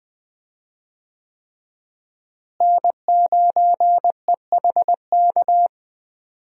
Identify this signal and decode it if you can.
Morse code, heard as a keyed tone, about 20 wpm, 700 Hz. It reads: N9EHK
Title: USA Call Signs - 5 Characters - Repeated 3x - 20wpm